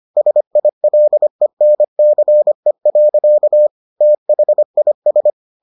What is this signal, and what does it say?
Morse code, heard as a keyed tone, about 25 wpm, 600 Hz. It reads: SILENCE. THIS